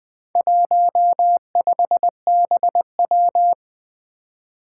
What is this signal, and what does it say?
Morse code, heard as a keyed tone, about 20 wpm, 700 Hz. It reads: 15BW
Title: Four Challenging Letter-Number Combinations 20wpm